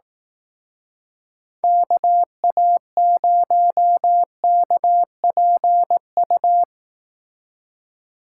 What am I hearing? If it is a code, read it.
Morse code, heard as a keyed tone, about 18 wpm, 700 Hz. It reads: KA0KPU
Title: Speed Racing - USA 2x3 Call Signs 15wpm